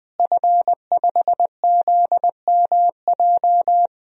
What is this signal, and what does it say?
Morse code, heard as a keyed tone, about 20 wpm, 700 Hz. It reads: F5ZMJ
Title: DXCC-100 Call Signs - 5 Characters - Repeated 2x - 20wpm